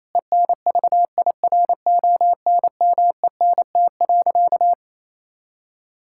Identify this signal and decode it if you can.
Morse code, heard as a keyed tone, about 28 wpm, 700 Hz. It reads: ENVIRONMENT.
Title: Sentences from Top 1000 Words - Review Part 5 - 28wpm